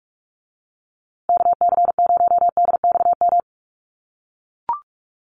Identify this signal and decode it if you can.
Morse code, heard as a keyed tone, about 44 wpm, 700 Hz. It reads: KC0DXM
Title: USA Call Signs Repeated 2x 45wpm